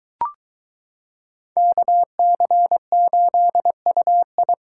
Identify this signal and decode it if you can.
Morse code, heard as a keyed tone, about 22 wpm, 700 Hz. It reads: KC8UI